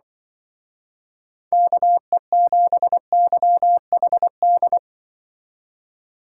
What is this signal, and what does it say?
Morse code, heard as a keyed tone, about 24 wpm, 700 Hz. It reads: KE7YHD
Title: Speed Racing - USA 2x3 Call Signs 20wpm